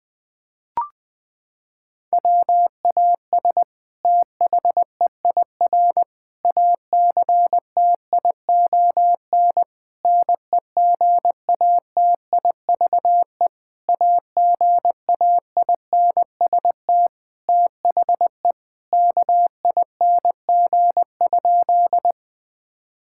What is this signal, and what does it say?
Morse code, heard as a keyed tone, about 20 wpm, 700 Hz. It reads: WAS THEIR ACTION NEGATIVE AGAINST THE KING?
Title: Sentences from Top 1000 Words - Review Part 6 - 20wpm